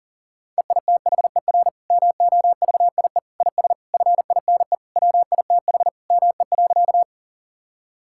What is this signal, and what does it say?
Morse code, heard as a keyed tone, about 40 wpm, 700 Hz. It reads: EITHER MOVIE IS FINE WITH ME.
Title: Sentences from Top 1000 Words - Review Part 5 - 40wpm